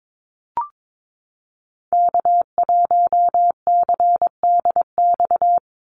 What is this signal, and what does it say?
Morse code, heard as a keyed tone, about 22 wpm, 700 Hz. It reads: K1CDX
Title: Call Signs - 22wpm